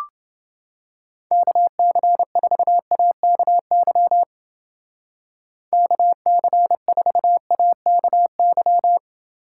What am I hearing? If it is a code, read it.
Morse code, heard as a keyed tone, about 29 wpm, 700 Hz. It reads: KC4AKY KC4AKY